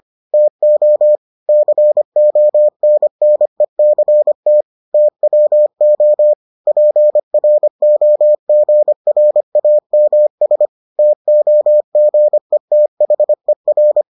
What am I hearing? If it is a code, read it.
Morse code, heard as a keyed tone, about 25 wpm, 600 Hz. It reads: TO CONNECT TWO PROGRAMS TOGETHER